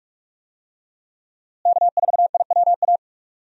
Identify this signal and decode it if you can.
Morse code, heard as a keyed tone, about 42 wpm, 700 Hz. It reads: K4IWA
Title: USA Call Signs - 5 Characters - Repeated 3x - 45wpm